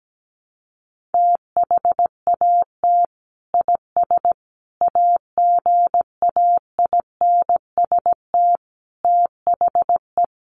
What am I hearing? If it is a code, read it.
Morse code, heard as a keyed tone, about 17 wpm, 700 Hz. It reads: THAT IS AGAINST THE
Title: Sentences from Top 1500 Words 17wpm